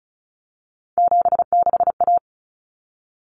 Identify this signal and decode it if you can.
Morse code, heard as a keyed tone, about 34 wpm, 700 Hz. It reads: 76A
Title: Common Cut Numbers - 3-Digit 35wpm